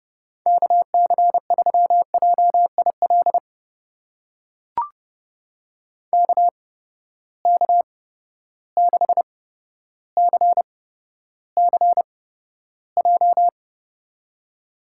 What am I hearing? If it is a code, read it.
Morse code, heard as a keyed tone, about 30 wpm, spaced as for 13 wpm, 700 Hz. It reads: KC3JIL KK6CCJ
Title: USA Call Signs - 6 Characters - 8x Character Spacing - 30wpm